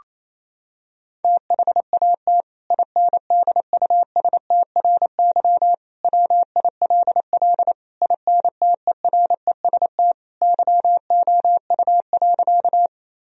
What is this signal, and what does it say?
Morse code, heard as a keyed tone, about 28 wpm, 700 Hz. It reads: THAT INDUSTRY WILL INTEREST YOU.
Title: Sentences from Top 1000 Words - Review Part 4 - 28wpm